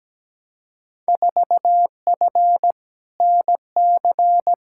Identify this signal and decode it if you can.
Morse code, heard as a keyed tone, about 17 wpm, 700 Hz. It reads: 4F NC
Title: Field Day Exchanges - 17wpm